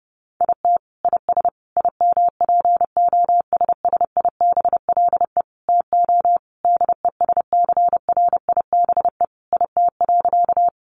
Morse code, heard as a keyed tone, about 30 wpm, 700 Hz. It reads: IT IS IMPOSSIBLE TO DESCRIBE IT.
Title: Sentences from Top 1100 Words 30wpm